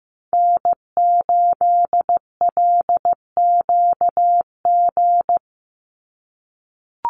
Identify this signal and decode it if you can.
Morse code, heard as a keyed tone, about 15 wpm, 700 Hz. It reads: N8LQG
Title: USA Call Signs 15wpm